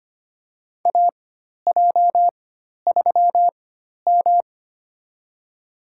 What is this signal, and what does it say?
Morse code, heard as a keyed tone, about 25 wpm, spaced as for 12 wpm, 700 Hz. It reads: AJ3M
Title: USA Call Signs - 4 Characters - 4x Character Spacing - 25wpm